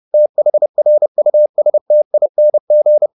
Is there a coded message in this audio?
Yes: Morse code from 0.1 to 3.1 s, heard as a keyed tone, about 30 wpm, 600 Hz. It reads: THRUSTING